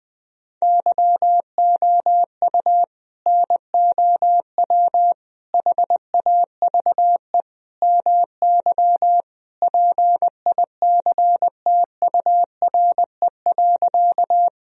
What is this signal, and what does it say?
Morse code, heard as a keyed tone, about 20 wpm, 700 Hz. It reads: YOU NOW HAVE MY PICTURE.
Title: Sentences from Top 1000 Words - Review Part 1 - 20wpm